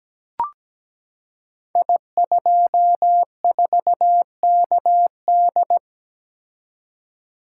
Morse code, heard as a keyed tone, about 17 wpm, 700 Hz. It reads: I24KD
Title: Code Groups - 5 Letter:Numbers - 17wpm